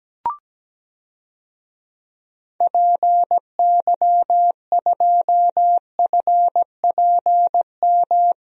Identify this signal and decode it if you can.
Morse code, heard as a keyed tone, about 17 wpm, 700 Hz. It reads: PY2FPM
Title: DXCC-100 Call Signs - 6 Characters - Repeated 2x - 17wpm